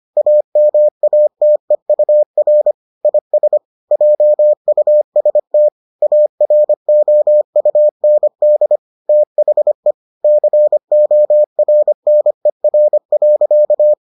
Morse code, heard as a keyed tone, about 25 wpm, 600 Hz. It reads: AMATEUR IS JUST AROUND THE CORNER.